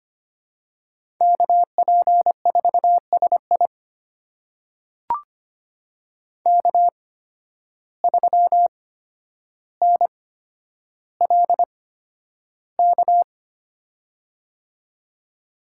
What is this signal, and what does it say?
Morse code, heard as a keyed tone, about 25 wpm, spaced as for 11 wpm, 700 Hz. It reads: KP4SI K3NLK
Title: USA Call Signs - 5 Characters - 8x Character Spacing - 25wpm